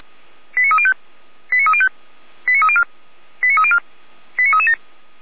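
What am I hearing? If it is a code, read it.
Five-tone selective call (ZVEI), heard as quick runs of five tones, tones of about 70 ms. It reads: 89305 89305 89304 89304 89337
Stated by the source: ZVEI five-tone call received by SDR.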